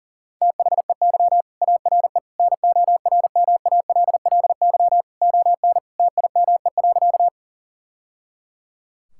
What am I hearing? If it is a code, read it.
Morse code, heard as a keyed tone, about 40 wpm, 700 Hz. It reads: THEY ARE NORMALLY ON TIME.
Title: Sentences from Top 1000 Words - Review Part 6 - 40wpm